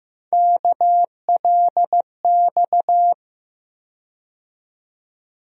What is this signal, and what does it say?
Morse code, heard as a keyed tone, about 15 wpm, 700 Hz. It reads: KLX
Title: USA Call Sign Postfixes - 3 Letters 15wpm